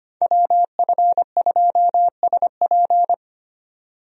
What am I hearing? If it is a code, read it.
Morse code, heard as a keyed tone, about 25 wpm, 700 Hz. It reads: WF2SP
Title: USA Call Signs - 5 Characters - Repeated 3x - 25wpm